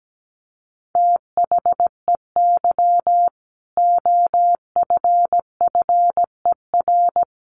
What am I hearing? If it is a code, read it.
Morse code, heard as a keyed tone, about 17 wpm, 700 Hz. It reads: THEY OFFER
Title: Sentences from Top 900 Words 17wpm